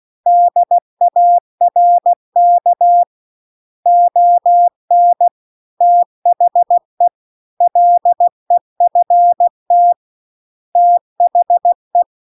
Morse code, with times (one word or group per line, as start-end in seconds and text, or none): 0.3-3.0 DARK
3.9-5.3 ON
5.8-7.1 THE
7.6-9.9 LEFT
10.8-12.0 THE